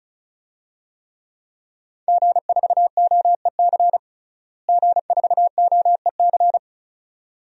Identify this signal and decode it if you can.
Morse code, heard as a keyed tone, about 34 wpm, 700 Hz. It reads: G4OEC G4OEC
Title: DXCC-100 Call Signs - 5 Characters - Repeated 2x - 35wpm